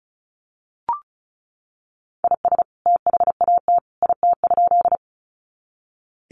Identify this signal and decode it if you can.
Morse code, heard as a keyed tone, about 34 wpm, 700 Hz. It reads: IS THAT IT?
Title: Sentences from Top 100 Words 35wpm